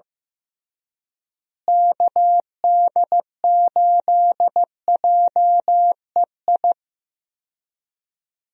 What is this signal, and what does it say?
Morse code, heard as a keyed tone, about 15 wpm, 700 Hz. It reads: KD8JEI
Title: USA Call Signs - 6 Characters - Repeated 3x - 15wpm